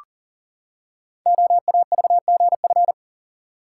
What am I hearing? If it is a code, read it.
Morse code, heard as a keyed tone, about 40 wpm, 690 Hz. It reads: OAVGF
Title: Code Groups - 5 Letters - 40wpm